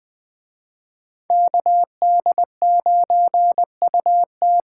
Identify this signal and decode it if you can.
Morse code, heard as a keyed tone, about 20 wpm, 700 Hz. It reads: KD9UT